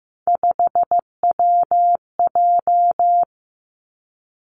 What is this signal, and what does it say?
Morse code, heard as a keyed tone, about 15 wpm, 700 Hz. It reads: 5WJ
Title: Three Challenging Letter-Number Combinations 15wpm